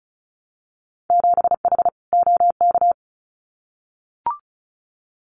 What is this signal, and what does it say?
Morse code, heard as a keyed tone, about 34 wpm, 700 Hz. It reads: 7H OK
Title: Winter Field Day Exchanges Encore 35wpm